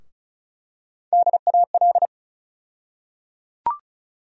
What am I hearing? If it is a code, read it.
Morse code, heard as a keyed tone, about 36 wpm, 700 Hz. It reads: DAL
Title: USA Call Sign Postfixes - 3 Letters 35wpm